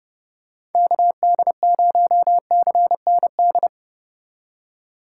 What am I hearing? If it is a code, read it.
Morse code, heard as a keyed tone, about 30 wpm, 700 Hz. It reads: KD0CND
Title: USA Call Signs - 4x Character Spacing - 30wpm